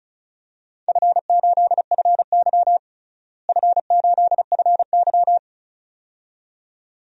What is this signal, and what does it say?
Morse code, heard as a keyed tone, about 34 wpm, 700 Hz. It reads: F8FY F8FY
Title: DXCC-100 Call Signs - 4 Characters - Repeated 2x - 35wpm